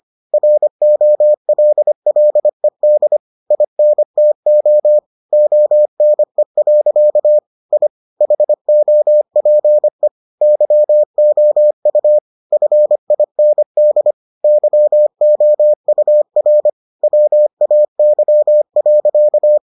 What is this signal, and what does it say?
Morse code, heard as a keyed tone, about 25 wpm, 600 Hz. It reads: ROLLED INTO ONE. I HOPE YOU FIND YOUR WAY.